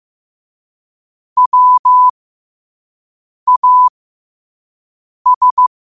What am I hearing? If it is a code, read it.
Morse code, heard as a keyed tone, about 15 wpm, spaced as for 5 wpm, 1000 Hz. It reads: WAS
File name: qotdfiles / 5wpm